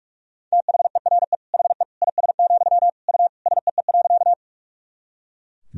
Morse code, heard as a keyed tone, about 41 wpm, 700 Hz. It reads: THERE HE IS, U SEE.
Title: Sentences from Top 100 Words and Abbreviations 45wpm